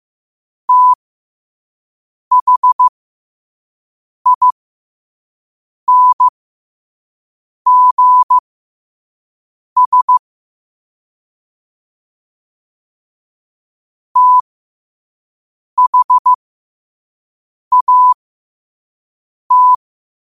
Morse code, heard as a keyed tone, about 15 wpm, spaced as for 5 wpm, 1000 Hz. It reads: THINGS THAT
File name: qotdfiles / 5wpm